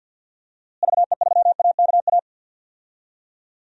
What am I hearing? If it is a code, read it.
Morse code, heard as a keyed tone, about 49 wpm, 700 Hz. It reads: VE3AKA
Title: DXCC-100 Call Signs - 6 Characters - Repeated 3x - 50wpm